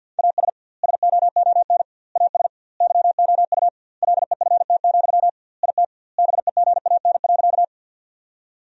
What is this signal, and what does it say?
Morse code, heard as a keyed tone, about 50 wpm, 700 Hz. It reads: AS SOON AS YOU LEFT, IT BEGAN.
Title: 2x Word-Spacing - Sentences from Top 400 Words - 50wpm